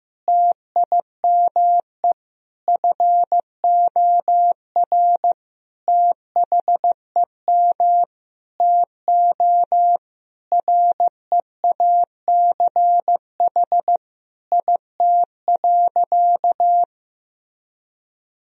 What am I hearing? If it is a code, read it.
Morse code, heard as a keyed tone, about 15 wpm, 700 Hz. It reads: TIME FOR THEM TO REACH IT.
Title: Sentences from Top 1000 Words - Review Part 2 - 15wpm